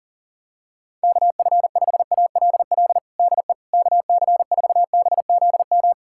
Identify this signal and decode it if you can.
Morse code, heard as a keyed tone, about 37 wpm, 700 Hz. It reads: KF5ALL DE KC4BZM
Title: USA Callsign DE Callsign - 6 Characters - 40wpm